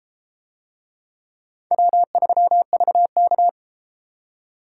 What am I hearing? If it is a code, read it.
Morse code, heard as a keyed tone, about 30 wpm, 700 Hz. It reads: W3VK